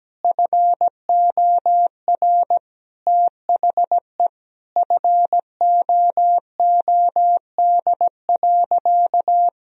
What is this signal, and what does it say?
Morse code, heard as a keyed tone, about 17 wpm, 700 Hz. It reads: FOR THE FOOD.
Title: Sentences from Top 1000 Words - Review Part 1 - 17wpm